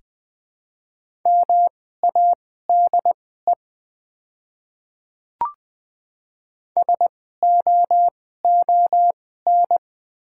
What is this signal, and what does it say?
Morse code, heard as a keyed tone, about 20 wpm, spaced as for 14 wpm, 690 Hz. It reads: MADE SOON
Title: 2x Character Spacing - Top 500 Words - 20wpm